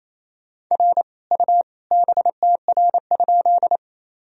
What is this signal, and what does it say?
Morse code, heard as a keyed tone, about 27 wpm, 700 Hz. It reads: R U BTR?
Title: Sentences from Top 400 Words and Abbreviations 28wpm